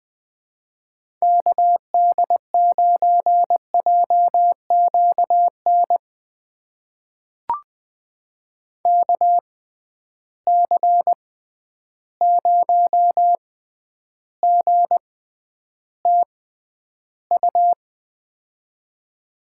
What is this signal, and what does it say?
Morse code, heard as a keyed tone, about 21 wpm, spaced as for 10 wpm, 700 Hz. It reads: KD9JQN KC0GTU
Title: USA Call Signs - 6 Characters - 6x Character Spacing - 20wpm